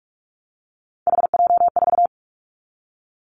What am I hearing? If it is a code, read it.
Morse code, heard as a keyed tone, about 46 wpm, 700 Hz. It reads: HJ4
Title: Callsign Prefixes for Top 100 DXCC - 45wpm